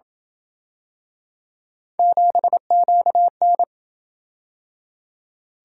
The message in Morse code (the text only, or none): 7QN